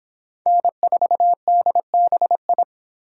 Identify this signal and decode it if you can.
Morse code, heard as a keyed tone, about 25 wpm, 700 Hz. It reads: N4DBI